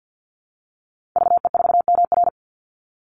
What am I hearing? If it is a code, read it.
Morse code, heard as a keyed tone, about 48 wpm, 700 Hz. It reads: VE4MR
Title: Call Signs Repeated 3x - 50wpm